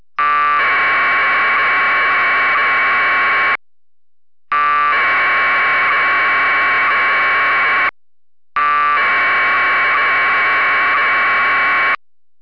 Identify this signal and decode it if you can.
Packet radio (AX.25, as in APRS).
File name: packet1200